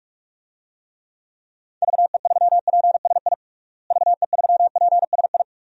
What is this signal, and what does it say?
Morse code, heard as a keyed tone, about 41 wpm, 700 Hz. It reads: VE3PSI VE3PSI
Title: DXCC-100 Call Signs - 6 Characters - Repeated 2x - 45wpm